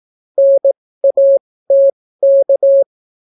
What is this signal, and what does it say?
Morse code, heard as a keyed tone, about 18 wpm, spaced as for 14 wpm, 550 Hz. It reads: NATK